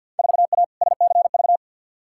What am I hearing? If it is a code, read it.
Morse code, heard as a keyed tone, about 52 wpm, 700 Hz. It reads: 4A SCV
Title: Field Day Exchanges - 50wpm